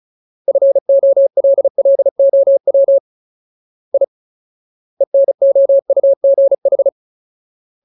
Morse code, heard as a keyed tone, about 34 wpm, 550 Hz. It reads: FOLLOW I ENOUGH
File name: Ultimate Word List / 35WPM Ultimate